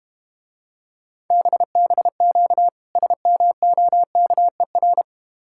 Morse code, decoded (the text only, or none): BBQ SMOKER